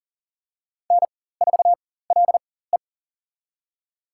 Morse code, heard as a keyed tone, about 40 wpm, spaced as for 19 wpm, 700 Hz. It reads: N4LE
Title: USA Call Signs - 4 Characters - 4x Character Spacing - 40wpm